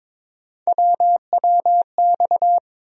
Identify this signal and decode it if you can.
Morse code, heard as a keyed tone, about 22 wpm, 700 Hz. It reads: WWX
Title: USA Call Sign Postfixes - 3 Letters 22wpm